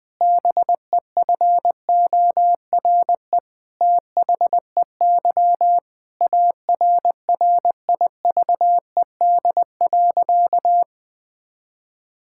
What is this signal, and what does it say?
Morse code, heard as a keyed tone, about 20 wpm, 700 Hz. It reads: BEFORE THEY ARRIVED.
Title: Sentences with 7-Word Phrases - 20wpm